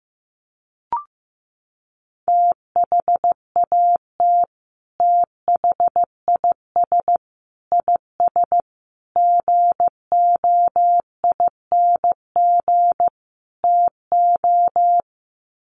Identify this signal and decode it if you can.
Morse code, heard as a keyed tone, about 15 wpm, 700 Hz. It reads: THAT THIS IS GOING TO
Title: Sets of 5 Words Encore2 15wpm